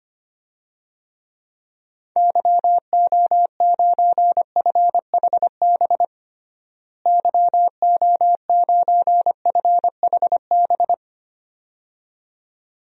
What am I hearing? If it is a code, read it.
Morse code, heard as a keyed tone, about 25 wpm, 700 Hz. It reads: YO9FHB YO9FHB
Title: DXCC-100 Call Signs - 6 Characters - Repeated 2x - 25wpm